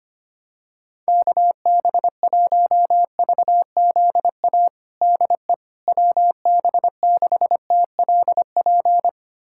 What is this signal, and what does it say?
Morse code, heard as a keyed tone, about 25 wpm, 700 Hz. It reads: KB1VZA DE WB6TLP
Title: USA Callsign DE Callsign - 6 Characters - 25wpm